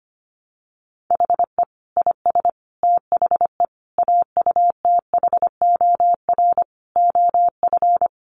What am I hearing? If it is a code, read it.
Morse code, heard as a keyed tone, about 25 wpm, 700 Hz. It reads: HE IS THE AUTHOR OF
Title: Sets of 5 Words Encore2 25wpm